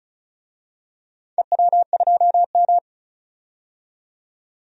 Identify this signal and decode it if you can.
Morse code, heard as a keyed tone, about 36 wpm, 700 Hz. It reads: EW2M
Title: DXCC-100 Call Signs - 4 Characters - 35wpm